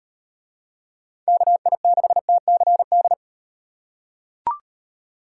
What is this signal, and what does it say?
Morse code, heard as a keyed tone, about 37 wpm, 700 Hz. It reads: KI6TCD